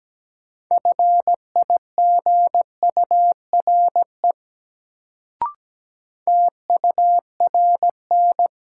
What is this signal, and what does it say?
Morse code, heard as a keyed tone, about 17 wpm, 700 Hz. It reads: FIGURE TURN
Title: Sets of 4 Words - New Word Drill - 17wpm